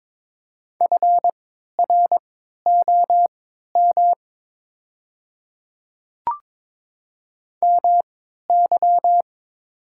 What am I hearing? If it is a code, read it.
Morse code, heard as a keyed tone, about 22 wpm, spaced as for 13 wpm, 700 Hz. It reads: FROM MY
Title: 3x Character Spacing - Top 100 Words - 22wpm